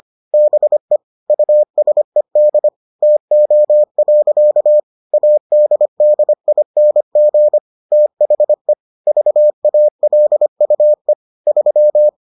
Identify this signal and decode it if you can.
Morse code, heard as a keyed tone, about 25 wpm, 600 Hz. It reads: BE USED TO. ADDING THE VALUE 3